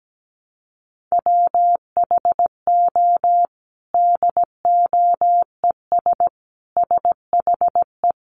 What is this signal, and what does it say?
Morse code, heard as a keyed tone, about 17 wpm, 700 Hz. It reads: WHO DOES SHE
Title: Sentences from Top 1300 Words 17wpm